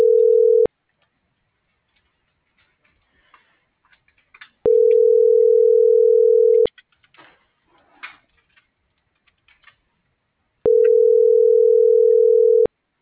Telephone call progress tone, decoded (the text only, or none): ringback tone